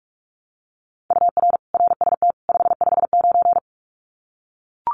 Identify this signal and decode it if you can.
Morse code, heard as a keyed tone, about 44 wpm, 700 Hz. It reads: UR RST 559
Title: Common QSO Phrases 45wpm